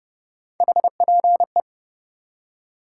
Morse code, heard as a keyed tone, about 37 wpm, 700 Hz. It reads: HPE
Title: Speed-Racing Top 90 QSO Elements 30wpm